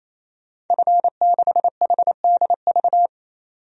Morse code, heard as a keyed tone, about 27 wpm, 700 Hz. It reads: F6HDV
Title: DXCC-100 Call Signs - 5 Characters - 28wpm